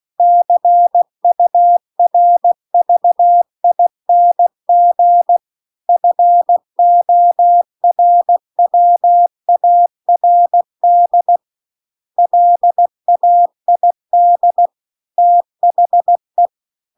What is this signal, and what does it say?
Morse code, heard as a keyed tone, about 16 wpm, 700 Hz. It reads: CURVING FORWARD LAID THE
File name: CW-The-Big-Twoheaded-River-2-16WPM